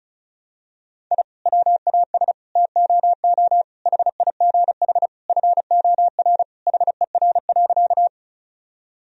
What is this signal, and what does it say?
Morse code, heard as a keyed tone, about 34 wpm, 700 Hz. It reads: I WAS TOO HIGH FOR HER.
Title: Sentences from Top 1000 Words - Review Part 1 - 35wpm